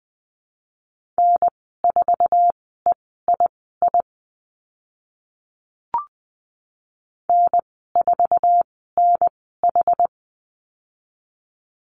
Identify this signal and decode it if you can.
Morse code, heard as a keyed tone, about 19 wpm, spaced as for 14 wpm, 700 Hz. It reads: N4EII N4NH
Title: 076 Lesson 13g - Letter L - Call Signs 20wpm